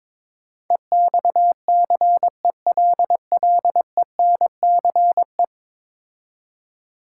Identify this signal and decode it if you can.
Morse code, heard as a keyed tone, about 21 wpm, 700 Hz. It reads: EXCELLENCE
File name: practice / 22wpm / 10 Letter Words - First 3 Letters Given - 22wpm